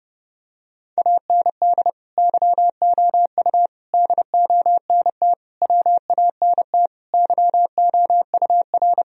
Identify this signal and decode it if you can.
Morse code, heard as a keyed tone, about 30 wpm, 700 Hz. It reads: AND YOU DONT WANT YOUR
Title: Sentences from Top 1000 Words - Review Part 2 - 30wpm